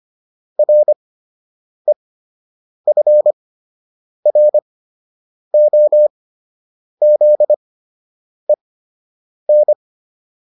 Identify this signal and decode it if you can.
Morse code, heard as a keyed tone, about 25 wpm, spaced as for 8 wpm, 600 Hz. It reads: REFROZEN